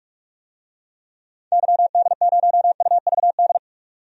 Morse code, heard as a keyed tone, about 46 wpm, 700 Hz. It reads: YD0UVD
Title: DXCC-100 Call Signs - 6 Characters - 45wpm